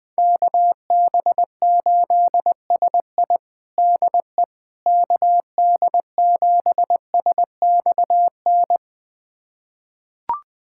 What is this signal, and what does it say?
Morse code, heard as a keyed tone, about 20 wpm, 700 Hz. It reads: KB8SI DE KD7SXN
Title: USA Callsign DE Callsign 20wpm